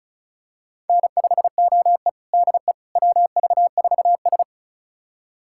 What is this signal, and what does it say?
Morse code, heard as a keyed tone, about 36 wpm, 700 Hz. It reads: N5OE DE WV4S
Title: USA Callsign DE Callsign - 4 Characters - 35wpm